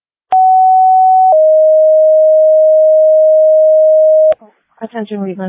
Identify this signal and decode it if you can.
Two-tone page, heard as a one-second tone, then a three-second tone, about 750 then 620 Hz, with speech. It is heard from 0.3 to 4.3 s.